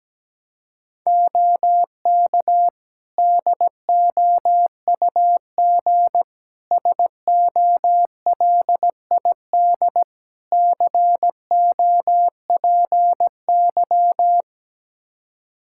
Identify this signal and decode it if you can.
Morse code, heard as a keyed tone, about 17 wpm, 700 Hz. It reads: OK DOUG SOLID COPY
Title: Practice QSO - Exchange 3 - More Info - 17wpm